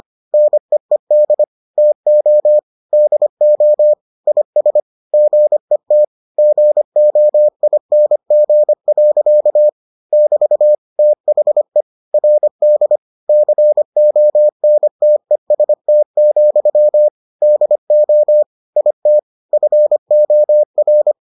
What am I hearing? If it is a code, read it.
Morse code, heard as a keyed tone, about 25 wpm, 600 Hz. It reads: NEED TO DO IS GET GOING. <BT> THE RD CONTEST, DO IT FOR